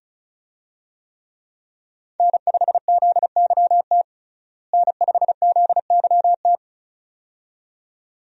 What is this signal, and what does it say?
Morse code, heard as a keyed tone, about 34 wpm, 700 Hz. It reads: N5ZYT N5ZYT
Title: USA Call Signs - 5 Characters - Repeated 2x - 35wpm